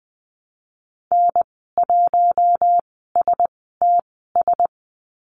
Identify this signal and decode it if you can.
Morse code, heard as a keyed tone, about 20 wpm, spaced as for 15 wpm, 700 Hz. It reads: N1STS
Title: 042 Lesson 8g - Number 1 - Call Signs 20wpm